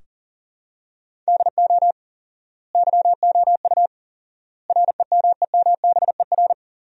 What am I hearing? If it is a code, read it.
Morse code, heard as a keyed tone, about 40 wpm, 700 Hz. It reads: DO YOU REMEMBER
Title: 4x Word-Spacing - Sentences from Top 400 Words - 40wpm